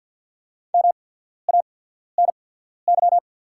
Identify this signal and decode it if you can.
Morse code, heard as a keyed tone, about 48 wpm, spaced as for 14 wpm, 690 Hz. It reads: MANY
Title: 8x Character Spacing - Top 100 Words - 50wpm